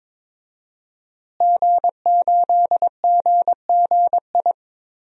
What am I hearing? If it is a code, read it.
Morse code, heard as a keyed tone, about 22 wpm, 700 Hz. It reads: G8GGI
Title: DXCC-100 Call Signs - 5 Characters - Repeated 2x - 22wpm